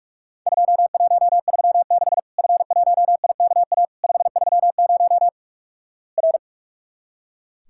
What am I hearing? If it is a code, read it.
Morse code, heard as a keyed tone, about 46 wpm, 700 Hz. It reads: 2136 F1IKA 530 R
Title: Worked All Europe DX Contest - QTC Exchange - 45wpm